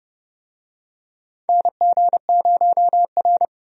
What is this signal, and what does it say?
Morse code, heard as a keyed tone, about 30 wpm, 700 Hz. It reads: NG0R
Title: USA Call Signs - 4 Characters - 30wpm